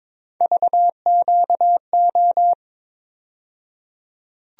Morse code, heard as a keyed tone, about 22 wpm, 700 Hz. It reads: VQO
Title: USA Call Sign Postfixes - 3 Letters 22wpm